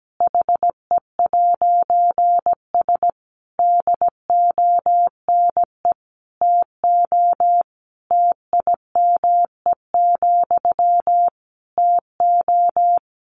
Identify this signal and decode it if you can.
Morse code, heard as a keyed tone, about 17 wpm, 700 Hz. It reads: HE'S DONE TO TIME, TO
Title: A Sound Of Thunder 17wpm